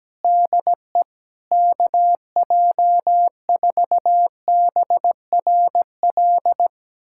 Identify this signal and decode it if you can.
Morse code, heard as a keyed tone, about 17 wpm, 700 Hz. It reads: DE KJ4BRL
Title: USA Callsign DE Callsign 17wpm